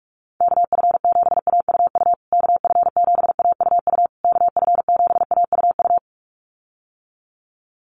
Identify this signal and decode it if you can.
Morse code, heard as a keyed tone, about 44 wpm, 700 Hz. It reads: KF7AUU KF7AUU KF7AUU
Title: USA Call Signs Repeated 3x 45wpm